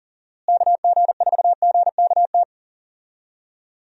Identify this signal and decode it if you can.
Morse code, heard as a keyed tone, about 37 wpm, 700 Hz. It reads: KG4GKT
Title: USA Call Signs - 6 Characters - Repeated 2x - 40wpm